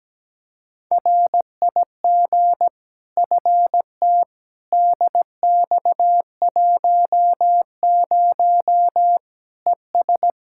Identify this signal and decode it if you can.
Morse code, heard as a keyed tone, about 17 wpm, 700 Hz. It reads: RIG FT DX10 ES
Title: K1USN SST QSO Practice 17wpm